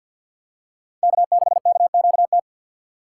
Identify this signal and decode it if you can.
Morse code, heard as a keyed tone, about 53 wpm, 700 Hz. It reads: K6KQT
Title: USA Call Signs - 5 Characters - 2x Character Spacing - 50wpm